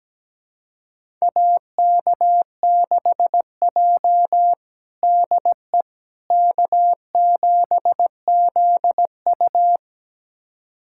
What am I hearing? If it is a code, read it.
Morse code, heard as a keyed tone, about 17 wpm, 700 Hz. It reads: AK6J DE K7ZU
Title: USA Callsign DE Callsign - 4 Characters - 17wpm